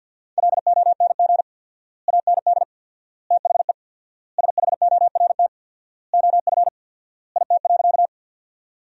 Morse code, heard as a keyed tone, about 50 wpm, 700 Hz. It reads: LONG AND THE SHORT OF IT.
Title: 4x Word-Spacing - Sentences from Top 400 Words - 50wpm